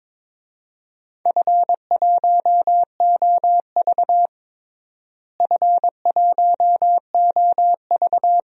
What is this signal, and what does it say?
Morse code, heard as a keyed tone, about 21 wpm, 700 Hz. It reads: F1OV F1OV
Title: DXCC-100 Call Signs - 4 Characters - Repeated 3x - 22wpm